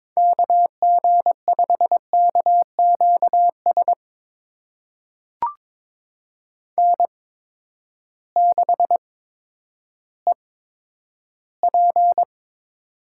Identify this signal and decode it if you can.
Morse code, heard as a keyed tone, about 22 wpm, spaced as for 11 wpm, 700 Hz. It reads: KG5KQS N6EP
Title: USA Call Signs - 8x Character Spacing - 22wpm